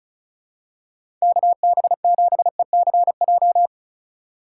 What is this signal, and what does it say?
Morse code, heard as a keyed tone, about 36 wpm, 700 Hz. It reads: KB7ECJ
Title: USA Call Signs - 6 Characters - 3x Character Spacing - 35wpm